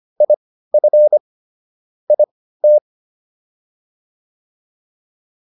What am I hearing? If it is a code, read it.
Morse code, heard as a keyed tone, about 25 wpm, spaced as for 15 wpm, 600 Hz. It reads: IF IT